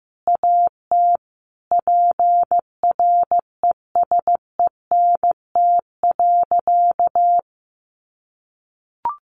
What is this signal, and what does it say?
Morse code, heard as a keyed tone, about 15 wpm, 700 Hz. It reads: AT PRESENT.
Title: Sentences from Top 1200 Words 15wpm